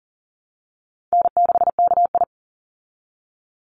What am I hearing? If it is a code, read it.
Morse code, heard as a keyed tone, about 40 wpm, 700 Hz. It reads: N6KI
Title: Call Signs Repeated 3x - 40wpm